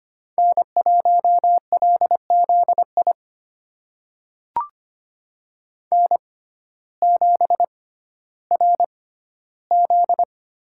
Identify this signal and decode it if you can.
Morse code, heard as a keyed tone, about 25 wpm, spaced as for 14 wpm, 700 Hz. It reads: N1LZI N7RZ
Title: USA Call Signs - 5 Characters - 6x Character Spacing - 25wpm